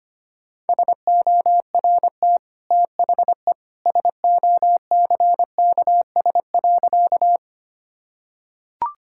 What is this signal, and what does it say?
Morse code, heard as a keyed tone, about 25 wpm, 700 Hz. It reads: SORT THE SOCKS.
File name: Easy Sentences with Hints - 25wpm